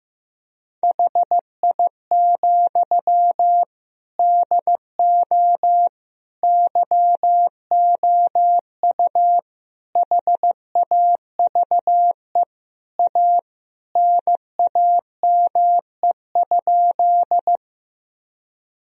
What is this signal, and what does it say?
Morse code, heard as a keyed tone, about 16 wpm, 700 Hz. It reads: HI, DO YOU HAVE A NAME?
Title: Sentences from Top 1000 Words - Review Part 3 - 15wpm